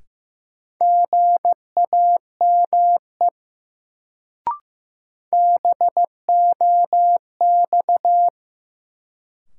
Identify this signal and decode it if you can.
Morse code, heard as a keyed tone, about 15 wpm, 700 Hz. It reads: GAME BOX
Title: Sentences from Top 500 Words - New Word Drill - 15wpm